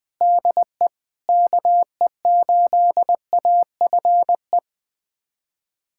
Morse code, heard as a keyed tone, about 20 wpm, 700 Hz. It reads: DE KE8AFE
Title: USA Callsign DE Callsign - 6 Characters - 20wpm